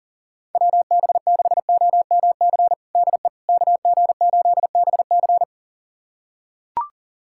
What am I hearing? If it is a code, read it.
Morse code, heard as a keyed tone, about 40 wpm, 700 Hz. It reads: WB6OMC DE KG8BC
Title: USA Callsign DE Callsign 40wpm